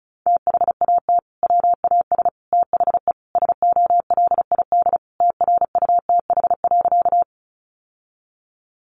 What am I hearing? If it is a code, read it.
Morse code, heard as a keyed tone, about 34 wpm, 700 Hz. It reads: THAT WAS THE SOLID TRUTH.
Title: Sentences from Top 1500 Words 35wpm